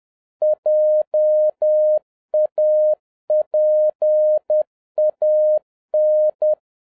Morse code, heard as a keyed tone, about 10 wpm, 610 Hz. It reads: JAPAN